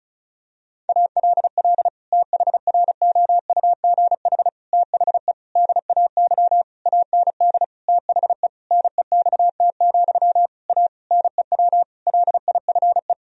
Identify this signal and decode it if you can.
Morse code, heard as a keyed tone, about 34 wpm, 700 Hz. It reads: ALL THROUGH THE DAY AND THE NEXT, A NEW LIFE
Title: Sentences from Top 1000 Words - Review Part 1 - 35wpm